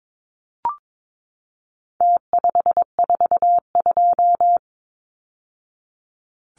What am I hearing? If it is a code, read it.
Morse code, heard as a keyed tone, about 22 wpm, 700 Hz. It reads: T542
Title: Numbers - Contest Style - 4-Digit 22wpm